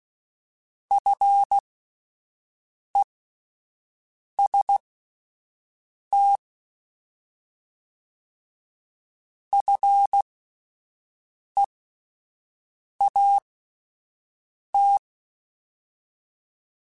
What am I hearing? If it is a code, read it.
Morse code, heard as a keyed tone, about 16 wpm, spaced as for 5 wpm, 800 Hz. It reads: FEST FEAT
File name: Alphabetical / Alphabetical mp3s / Fs @5